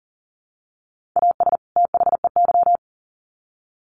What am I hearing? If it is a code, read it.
Morse code, heard as a keyed tone, about 40 wpm, 700 Hz. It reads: AS THEY
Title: Sets of 2 Words 40wpm